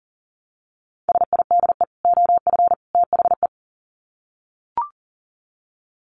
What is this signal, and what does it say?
Morse code, heard as a keyed tone, about 40 wpm, 700 Hz. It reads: SIDE OF THE
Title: Sets of 3 Words 40wpm